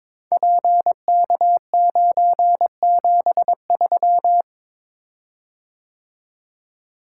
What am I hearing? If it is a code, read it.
Morse code, heard as a keyed tone, about 23 wpm, 700 Hz. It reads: PK973
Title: Code Groups - 5 Letter:Numbers - 22wpm